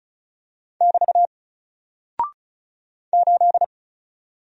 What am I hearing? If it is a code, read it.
Morse code, heard as a keyed tone, about 36 wpm, 700 Hz. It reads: <BT> 8
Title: Single Character (Letter:Number:Punctuation:Prosign) Alt 35wpm